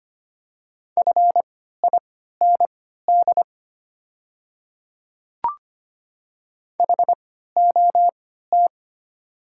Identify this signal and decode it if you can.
Morse code, heard as a keyed tone, about 25 wpm, spaced as for 14 wpm, 700 Hz. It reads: FIND HOT
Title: 3x Character Spacing - Top 100 Words - 25wpm